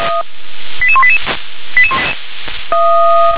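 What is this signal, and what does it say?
Five-tone selective call (ZVEI), heard as quick runs of five tones, tones of about 70 ms. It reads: 88188
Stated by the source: Received on an SDR; ZVEI five-tone selective call, code 88202.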